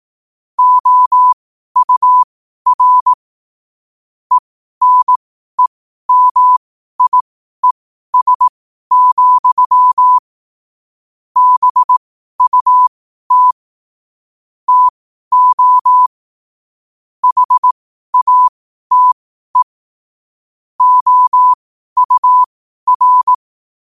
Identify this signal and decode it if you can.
Morse code, heard as a keyed tone, about 18 wpm, spaced as for 12 wpm, 1000 Hz. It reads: OUR ENEMIES, BUT TO HATE OUR
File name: qotdfiles / 12wpm